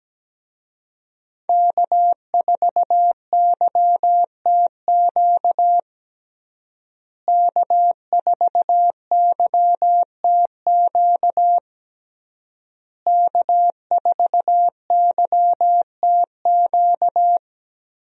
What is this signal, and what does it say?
Morse code, heard as a keyed tone, about 17 wpm, 700 Hz. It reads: K4YTQ K4YTQ K4YTQ
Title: USA Call Signs - 5 Characters - Repeated 3x - 17wpm